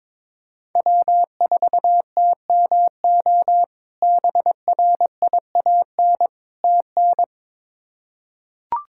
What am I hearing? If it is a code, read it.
Morse code, heard as a keyed tone, about 21 wpm, 700 Hz. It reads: W4TMO BRIAN TN
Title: K1USN SST QSO Practice - 22wpm